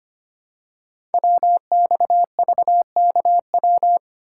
Morse code, heard as a keyed tone, about 25 wpm, 700 Hz. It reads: WXVKW
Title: Code Groups - 5 Letters - 25wpm